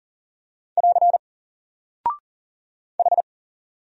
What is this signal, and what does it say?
Morse code, heard as a keyed tone, about 40 wpm, 700 Hz. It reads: <AR> H
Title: Single Character (Letter:Number:Punctuation:Prosign) 40wpm